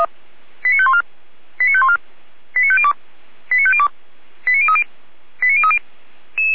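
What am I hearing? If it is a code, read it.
Five-tone selective call (ZVEI), heard as quick runs of five tones, tones of about 70 ms. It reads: 89514 89514 89502 89502 89030 89030
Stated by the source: ZVEI five-tone selective call 89099 received via SDR, from the Feuerwehr Orlamünde (feuerwehr-orlamuende.org) audio server.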